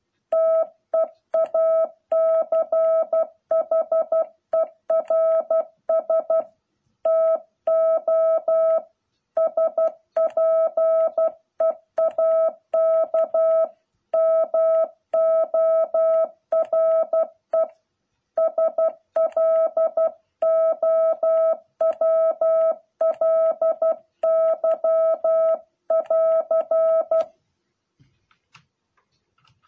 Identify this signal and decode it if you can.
Morse code, heard as a keyed tone, about 12 wpm, 660 Hz. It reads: TEACHERS TO SPEAK MORE SLOWLY <AR>